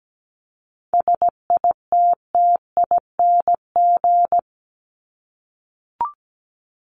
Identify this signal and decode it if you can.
Morse code, heard as a keyed tone, about 17 wpm, 700 Hz. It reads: SITTING
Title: ING Suffix - Top 100 Words with Base Words 17wpm